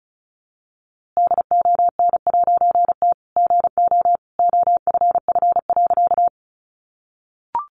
Morse code, heard as a keyed tone, about 34 wpm, 700 Hz. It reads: DON'T GO OFF.
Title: A Sound Of Thunder 35wpm